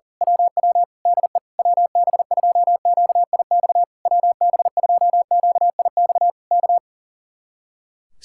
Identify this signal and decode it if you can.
Morse code, heard as a keyed tone, about 38 wpm, 700 Hz. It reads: WW DE WB2QIX WB2QIX K
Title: Speed-Racing Common QSO Phrases Encore - 40wpm